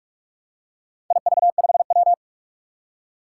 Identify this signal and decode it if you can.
Morse code, heard as a keyed tone, about 44 wpm, 700 Hz. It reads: IV5W
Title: Four Challenging Letter-Number Combinations 45wpm